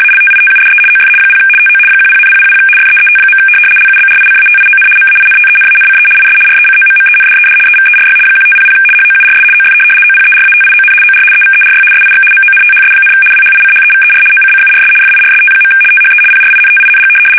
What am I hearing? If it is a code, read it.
RTTY, about 75 baud. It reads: ALTITUDE OF 0 TO 10,000 FEET. LUBRICATION MAINTENANCE IS RECOMMENDED EVERY 1500 HOURS OF OPERATION OR EVERY SIX MONTHS, WHICHEVER OCCURS FIRST. THE PRINTING
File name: fsk850-75B / teletype-corp